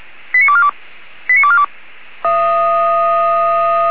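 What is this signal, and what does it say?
Five-tone selective call (ZVEI), heard as quick runs of five tones, tones of about 70 ms. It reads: 89242 89242
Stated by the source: ZVEI five-tone call received on SDR.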